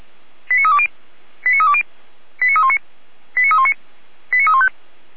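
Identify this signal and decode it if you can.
Five-tone selective call (ZVEI), heard as quick runs of five tones, tones of about 80 ms. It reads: 89320 89320 89319 89319 89315
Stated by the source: Five-tone ZVEI selective call received by SDR.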